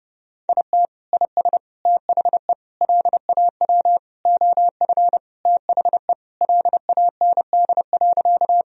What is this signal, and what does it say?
Morse code, heard as a keyed tone, about 30 wpm, 700 Hz. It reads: IT IS THE LAW OF THE LAND.
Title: Sentences from Top 1000 Words - Review Part 3 - 30wpm